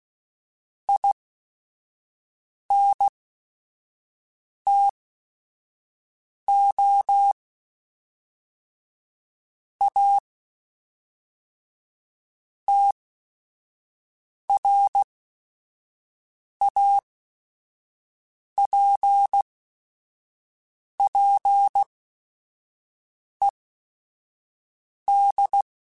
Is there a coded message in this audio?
Yes: Morse code from 0.9 to 25.6 s, heard as a keyed tone, about 16 wpm, spaced as for 5 wpm, 800 Hz. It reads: INTO A TRAPPED